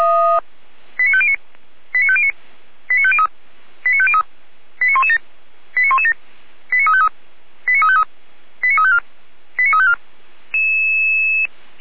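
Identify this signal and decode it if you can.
Five-tone selective call (ZVEI), heard as quick runs of five tones, tones of about 70 ms. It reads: 89509 89509 89503 89503 89117 89117 89353 89353 89354 89354
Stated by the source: ZVEI five-tone call 89164 received on an SDR, hosted by Feuerwehr Orlamünde.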